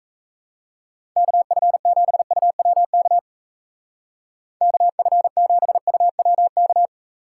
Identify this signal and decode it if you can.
Morse code, heard as a keyed tone, about 35 wpm, 700 Hz. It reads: KF7UWK KF7UWK
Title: Speed Racing - USA 2x3 Call Signs 28wpm